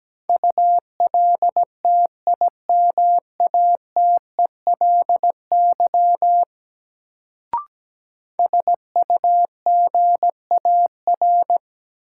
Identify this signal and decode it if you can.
Morse code, heard as a keyed tone, about 17 wpm, 700 Hz. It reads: ULTIMATELY SUGAR
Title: Sentences from Top 1200 Words - New Word Drill Alt - 17wpm